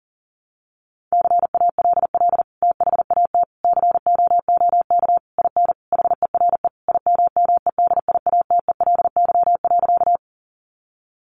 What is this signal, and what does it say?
Morse code, heard as a keyed tone, about 37 wpm, 700 Hz. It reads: CALL THAT COOK IN HERE IMMEDIATELY.
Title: Sentences from Top 1100 Words 40wpm